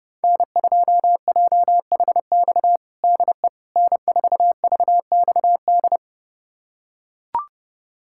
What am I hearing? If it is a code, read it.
Morse code, heard as a keyed tone, about 30 wpm, 700 Hz. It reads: N2JHX DE N4VXD
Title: USA Callsign DE Callsign - 5 Characters - 30wpm